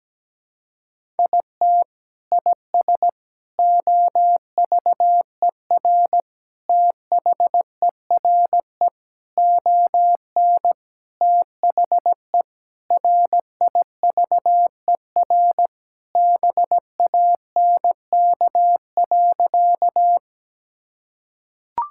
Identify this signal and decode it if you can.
Morse code, heard as a keyed tone, about 17 wpm, 700 Hz. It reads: IT IS OVER THERE ON THE RIVER BANK.
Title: Sentences from Top 1000 Words - Review Part 6 - 17wpm